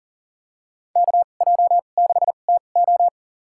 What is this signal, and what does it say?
Morse code, heard as a keyed tone, about 37 wpm, spaced as for 28 wpm, 700 Hz. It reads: KJ6TO
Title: USA Call Signs - 5 Characters - 2x Character Spacing - 40wpm